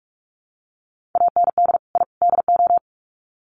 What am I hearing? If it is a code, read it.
Morse code, heard as a keyed tone, about 44 wpm, 700 Hz. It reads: AND I DO
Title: Sets of 3 Words 45wpm